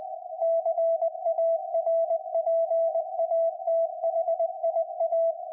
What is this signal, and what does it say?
Morse code, heard as a keyed tone, about 20 wpm, 660 Hz. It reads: CARPATHIA
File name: IC-7100